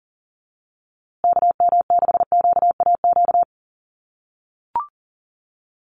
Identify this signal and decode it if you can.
Morse code, heard as a keyed tone, about 40 wpm, 700 Hz. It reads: KM6QAQ
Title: USA Call Signs Repeated 3x 40wpm